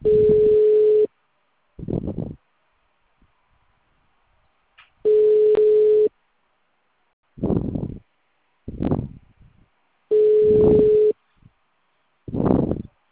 Ringback tone (Europe, 425 Hz).
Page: Sala 906 Ribeirao Preto - SP